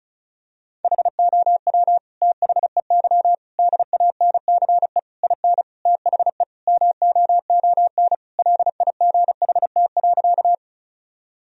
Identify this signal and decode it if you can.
Morse code, heard as a keyed tone, about 36 wpm, 700 Hz. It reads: HOW THEY DANCE IN THE MOON LIGHT.
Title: Sentences from Top 1000 Words - Review Part 4 - 35wpm